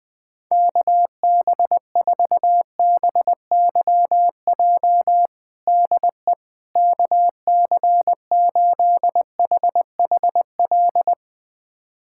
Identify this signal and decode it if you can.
Morse code, heard as a keyed tone, about 20 wpm, 700 Hz. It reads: KB4BYJ DE KC8HHL
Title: USA Callsign DE Callsign 20wpm